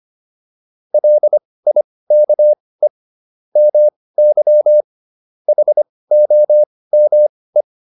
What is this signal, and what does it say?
Morse code, heard as a keyed tone, about 25 wpm, spaced as for 18 wpm, 600 Hz. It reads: LIKE MY HOME
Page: intermediate-practice-files